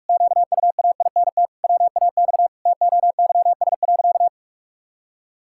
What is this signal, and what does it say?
Morse code, heard as a keyed tone, about 41 wpm, 700 Hz. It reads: QUAINT WAX TOYS.
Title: Pangram Challenge v2 - 45wpm